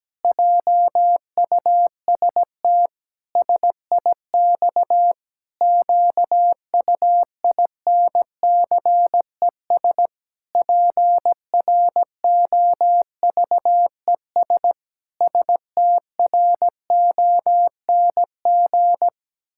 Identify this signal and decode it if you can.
Morse code, heard as a keyed tone, about 17 wpm, 700 Hz. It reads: JUST SIX QUINCES PROVES STRONG
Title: Pangram Challenge v2 - 17wpm